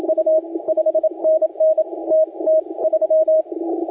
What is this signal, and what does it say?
Morse code, heard as a keyed tone, about 27 wpm, 630 Hz. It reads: U 5NN T T 3
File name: la4c / sac-cw-2018 / CONTEST AUDIO 2200Z